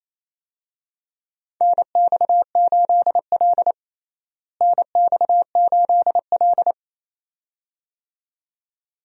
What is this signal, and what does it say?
Morse code, heard as a keyed tone, about 28 wpm, 700 Hz. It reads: NX8L NX8L
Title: USA Call Signs - 4 Characters - Repeated 2x - 28wpm